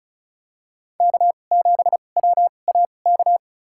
Morse code, heard as a keyed tone, about 36 wpm, spaced as for 25 wpm, 700 Hz. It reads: K7WAK
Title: USA Call Signs - 5 Characters - 2x Character Spacing - 35wpm